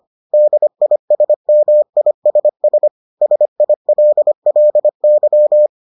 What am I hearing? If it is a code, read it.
Morse code, heard as a keyed tone, about 25 wpm, 600 Hz. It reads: DISMISS SILLY